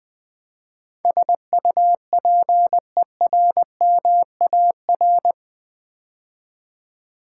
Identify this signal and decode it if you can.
Morse code, heard as a keyed tone, about 20 wpm, 700 Hz. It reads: SUPERMAR
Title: Build A Word - 4 Letters At A Time - 11-Letter Words - 20wpm